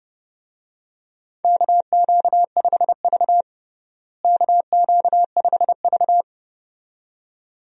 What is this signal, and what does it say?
Morse code, heard as a keyed tone, about 31 wpm, 700 Hz. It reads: KQ5V KQ5V
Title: USA Call Signs - 4 Characters - Repeated 2x - 30wpm